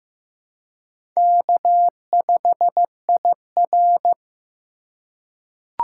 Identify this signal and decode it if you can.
Morse code, heard as a keyed tone, about 15 wpm, 700 Hz. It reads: K5IR
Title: USA Call Signs - 4 Characters - 2x Character Spacing - 15wpm